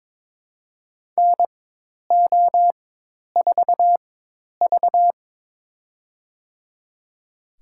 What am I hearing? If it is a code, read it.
Morse code, heard as a keyed tone, about 22 wpm, spaced as for 10 wpm, 700 Hz. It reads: NO4V
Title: USA Call Signs - 4x Character Spacing - 22wpm